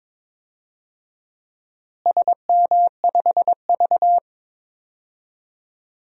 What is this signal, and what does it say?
Morse code, heard as a keyed tone, about 22 wpm, 700 Hz. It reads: SM5V
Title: DXCC-100 Call Signs - 4 Characters - Repeated 2x - 22wpm